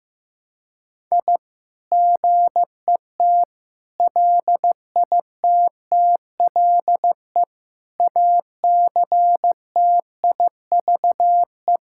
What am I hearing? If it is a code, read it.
Morse code, heard as a keyed tone, about 15 wpm, 700 Hz. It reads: I GET LITTLE ACTIVE
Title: Sentences from Top 1000 Words - Review Part 6 - 15wpm